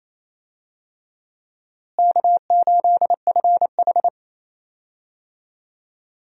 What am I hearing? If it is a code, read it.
Morse code, heard as a keyed tone, about 27 wpm, 700 Hz. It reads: K8FH
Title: USA Call Signs - 4 Characters - Repeated 3x - 28wpm